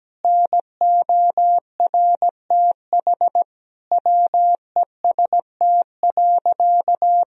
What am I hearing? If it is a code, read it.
Morse code, heard as a keyed tone, about 17 wpm, 700 Hz. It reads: NORTH WEST.
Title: Sentences from Top 1000 Words - Review Part 2 - 17wpm